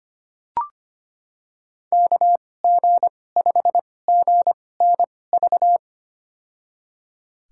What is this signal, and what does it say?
Morse code, heard as a keyed tone, about 25 wpm, spaced as for 18 wpm, 700 Hz. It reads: KG5GNV
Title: USA Call Signs - 2x Character Spacing - 25wpm